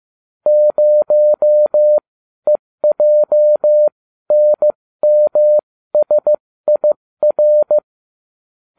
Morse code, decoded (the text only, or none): 0EJNMSIR